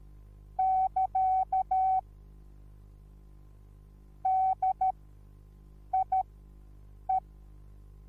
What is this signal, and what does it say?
Morse code, heard as a keyed tone, about 13 wpm, spaced as for 7 wpm, 760 Hz. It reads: <KA> DIE
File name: Onlinelehrgang / morsen / DJ4UF Morsen21